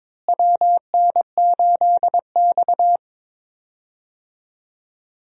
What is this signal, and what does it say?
Morse code, heard as a keyed tone, about 21 wpm, 700 Hz. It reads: WN8X
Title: USA Call Signs - 4 Characters - Repeated 3x - 22wpm